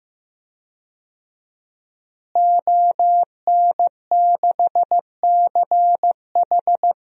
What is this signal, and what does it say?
Morse code, heard as a keyed tone, about 16 wpm, 700 Hz. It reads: ON6CH
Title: DXCC-100 Call Signs - 5 Characters - Repeated 3x - 15wpm